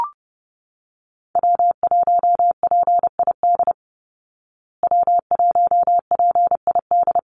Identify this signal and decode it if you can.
Morse code, heard as a keyed tone, about 30 wpm, 700 Hz. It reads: W1PID W1PID
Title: Call Signs Repeated 3x - 30wpm